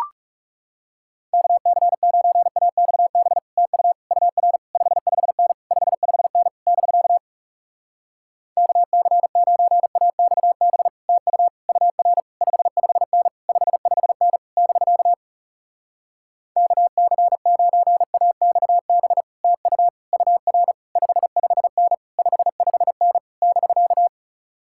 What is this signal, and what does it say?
Morse code, heard as a keyed tone, about 40 wpm, 700 Hz. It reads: KC9AXB TU UR 55N 55N <BK> KC9AXB TU UR 55N 55N <BK> KC9AXB TU UR 55N 55N <BK>